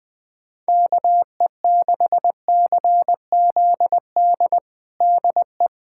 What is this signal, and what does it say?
Morse code, heard as a keyed tone, about 20 wpm, 700 Hz. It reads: KE6CZD DE
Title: USA Callsign DE Callsign - 6 Characters - 20wpm